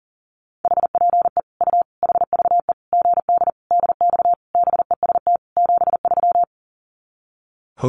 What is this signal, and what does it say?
Morse code, heard as a keyed tone, about 40 wpm, 700 Hz. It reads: HPE U HVE GD DX BEST 73
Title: Common QSO Phrases 40wpm